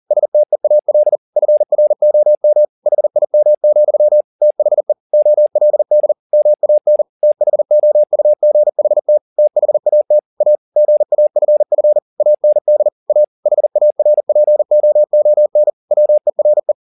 Morse code, heard as a keyed tone, about 40 wpm, 600 Hz. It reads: STEAL FROM HIM, THE OLD MAN THOUGHT THAT A GAFF AND A HARPOON WERE